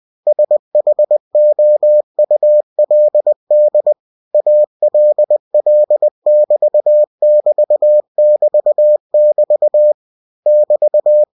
Morse code, heard as a keyed tone, about 20 wpm, 600 Hz. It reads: SHOULD ALL<BT><BT><BT><BT> <BT>
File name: cw-books / winnie-the-pooh / winnie-20wpm-page-0107